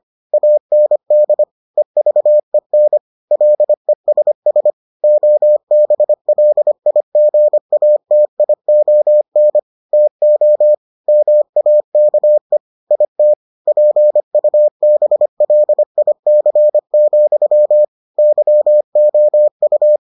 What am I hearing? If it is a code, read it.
Morse code, heard as a keyed tone, about 25 wpm, 600 Hz. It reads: AND EVEN LESS OBLIGATION TO MAKE IT PUBLIC, YOU